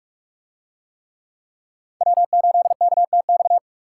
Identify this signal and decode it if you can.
Morse code, heard as a keyed tone, about 44 wpm, 700 Hz. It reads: W8KTX